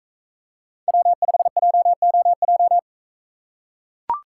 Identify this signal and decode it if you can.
Morse code, heard as a keyed tone, about 41 wpm, 700 Hz. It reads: W5JOJ